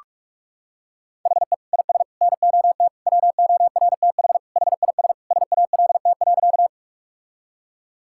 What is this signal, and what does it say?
Morse code, heard as a keyed tone, about 46 wpm, 700 Hz. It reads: HE IS NOT WORTH HIS SALT.
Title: Sentences from Top 1000 Words - Review Part 5 - 45wpm